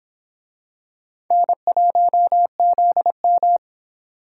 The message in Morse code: N1ZM

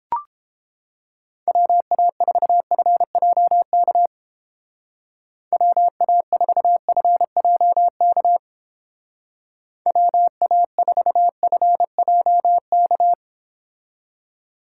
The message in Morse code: WA4FJK WA4FJK WA4FJK

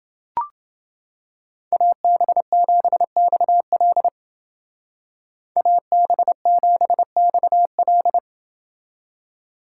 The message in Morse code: AB7XL AB7XL

